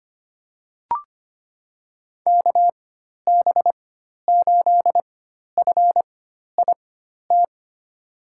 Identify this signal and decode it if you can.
Morse code, heard as a keyed tone, about 25 wpm, spaced as for 12 wpm, 700 Hz. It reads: KB8FIT
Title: USA Call Signs - 6 Characters - 4x Character Spacing - 25wpm